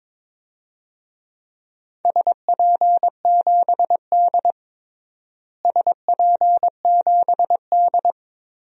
Morse code, heard as a keyed tone, about 22 wpm, 700 Hz. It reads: SP7D SP7D
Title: DXCC-100 Call Signs - 4 Characters - Repeated 3x - 22wpm